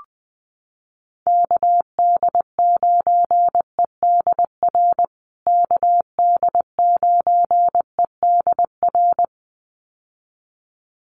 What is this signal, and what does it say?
Morse code, heard as a keyed tone, about 20 wpm, 700 Hz. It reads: KD9EDR KD9EDR
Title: USA Call Signs Repeated 2x 20wpm